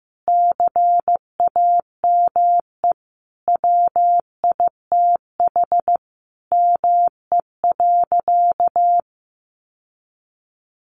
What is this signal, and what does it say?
Morse code, heard as a keyed tone, about 15 wpm, 700 Hz. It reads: CAME WITH ME.
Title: Sentences from Top 1300 Words 15wpm